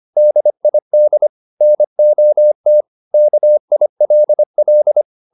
Morse code, heard as a keyed tone, about 25 wpm, 600 Hz. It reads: DID NOT KILL